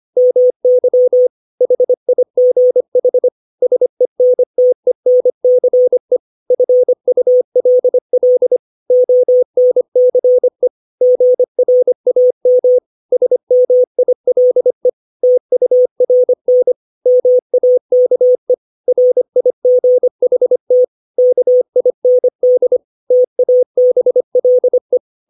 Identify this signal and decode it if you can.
Morse code, heard as a keyed tone, about 25 wpm, 500 Hz. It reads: MY HIGH SENTENCE FULL ONCE GRAM SMILE TURN MAKE RIGHT KIND TABLE
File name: 632 words / 632 Words 25WPM